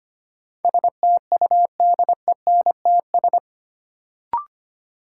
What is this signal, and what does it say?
Morse code, heard as a keyed tone, about 25 wpm, 700 Hz. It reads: STUDENTS
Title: Sets of 5 Words Encore - New Word Drill - 25wpm